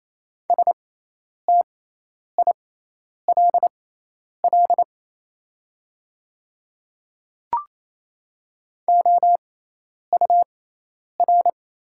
Morse code, heard as a keyed tone, about 28 wpm, spaced as for 10 wpm, 700 Hz. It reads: STILL OUR
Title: 6x Character Spacing - Top 300 Words - 28wpm